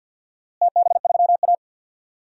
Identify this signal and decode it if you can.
Morse code, heard as a keyed tone, about 52 wpm, 700 Hz. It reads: TB3A
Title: DXCC-100 Call Signs - 4 Characters - Repeated 2x - 50wpm